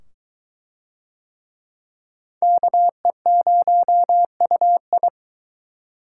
Morse code, heard as a keyed tone, about 22 wpm, 700 Hz. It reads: KE0UI